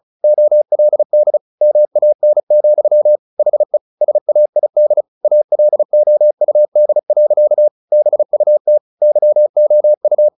Morse code, heard as a keyed tone, about 34 wpm, 600 Hz. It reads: OLD MAN, HE SAID ALOUD. BUT YOU
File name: the old man and the sea 2 35wpm